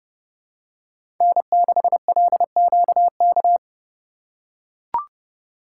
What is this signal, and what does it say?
Morse code, heard as a keyed tone, about 32 wpm, 700 Hz. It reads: N6LQK
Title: USA Call Signs - 5 Characters - 2x Character Spacing - 30wpm